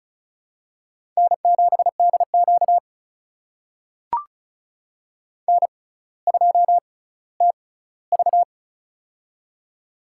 Morse code, heard as a keyed tone, about 34 wpm, spaced as for 18 wpm, 700 Hz. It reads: N7DQ N2TV
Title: USA Call Signs - 4 Characters - 6x Character Spacing - 35wpm